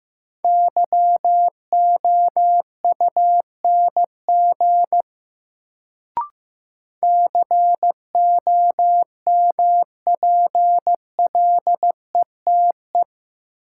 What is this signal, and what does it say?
Morse code, heard as a keyed tone, about 15 wpm, 700 Hz. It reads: YOUNG COMPLETE
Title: Sentences from Top 400 Words - New Word Drill - 15wpm